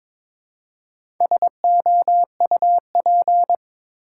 Morse code, heard as a keyed tone, about 22 wpm, 700 Hz. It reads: SOUP
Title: Four Letter Words - 22wpm